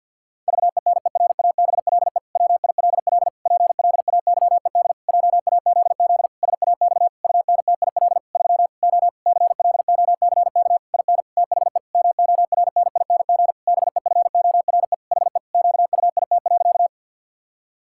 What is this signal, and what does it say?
Morse code, heard as a keyed tone, about 52 wpm, 700 Hz. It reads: VENERABLE WILL PLAYED JAZZ SAX UNTIL 3 O CLOCK IN THE MORNING BEFORE HE QUIT.
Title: Pangram Challenge v2 - 50wpm